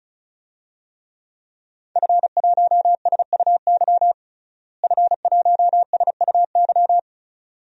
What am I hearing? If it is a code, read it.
Morse code, heard as a keyed tone, about 34 wpm, 700 Hz. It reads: F1SUY F1SUY
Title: DXCC-100 Call Signs - 5 Characters - Repeated 2x - 35wpm